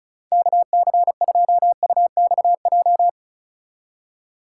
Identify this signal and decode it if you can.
Morse code, heard as a keyed tone, about 33 wpm, 700 Hz. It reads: KC2UXJ
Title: Speed Racing - USA Call Signs 35wpm